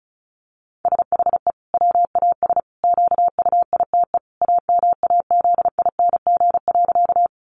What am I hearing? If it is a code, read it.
Morse code, heard as a keyed tone, about 34 wpm, 700 Hz. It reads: SHE WAS QUITE AMAZING.
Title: Sentences from Top 1500 Words 35wpm